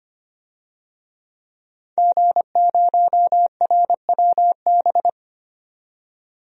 Morse code, heard as a keyed tone, about 25 wpm, 700 Hz. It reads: G0RWB
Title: DXCC-100 Call Signs - 5 Characters - 25wpm